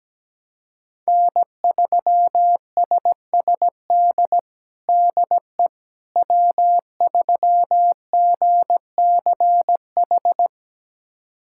Morse code, heard as a keyed tone, about 17 wpm, 700 Hz. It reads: N3SSD DE W3GCH
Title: USA Callsign DE Callsign - 5 Characters - 17wpm